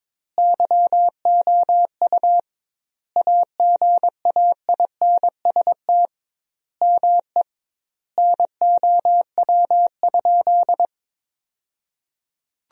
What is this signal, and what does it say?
Morse code, heard as a keyed tone, about 22 wpm, 700 Hz. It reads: YOU AGAINST ME NOW?
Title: 2x Word-Spacing - Sentences from Top 400 Words - 22wpm